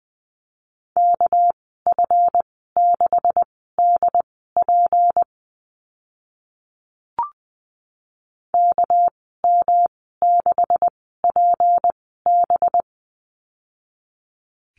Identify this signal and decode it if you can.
Morse code, heard as a keyed tone, about 19 wpm, spaced as for 14 wpm, 700 Hz. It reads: KF6DP KM6PB
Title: 207 Lesson 33g - Letter j - Call Signs 20wpm